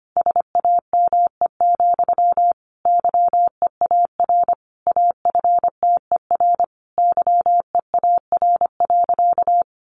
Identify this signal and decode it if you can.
Morse code, heard as a keyed tone, about 25 wpm, 700 Hz. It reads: SAME, YEAR AFTER YEAR.
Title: Sentences from Top 300 Words 25wpm